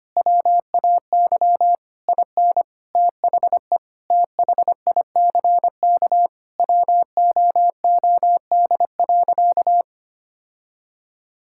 Morse code, heard as a keyed tone, about 25 wpm, 700 Hz. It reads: WAY IN THE THICK WOOD.
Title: Sentences from Top 1600 Words 25wpm